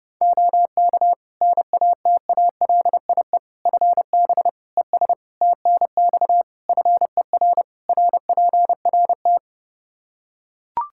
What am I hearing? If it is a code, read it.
Morse code, heard as a keyed tone, about 30 wpm, 700 Hz. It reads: OK NATALIE FB ES TNX FER RPRT
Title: Practice QSO - Exchange 2 - Info - 30wpm